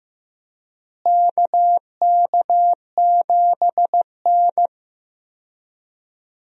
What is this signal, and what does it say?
Morse code, heard as a keyed tone, about 15 wpm, 700 Hz. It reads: KK7N
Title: USA Call Signs - 4 Characters - Repeated 3x - 15wpm